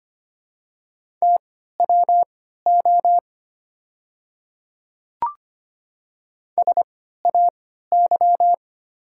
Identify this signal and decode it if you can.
Morse code, heard as a keyed tone, about 25 wpm, spaced as for 14 wpm, 700 Hz. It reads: TWO SAY
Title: 3x Character Spacing - Top 200 Words - 25wpm